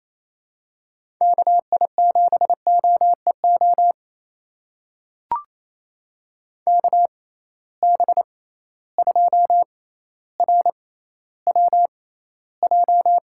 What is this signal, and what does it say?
Morse code, heard as a keyed tone, about 28 wpm, spaced as for 14 wpm, 700 Hz. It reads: KI7OEO KB2RWJ
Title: USA Call Signs - 6 Characters - 6x Character Spacing - 28wpm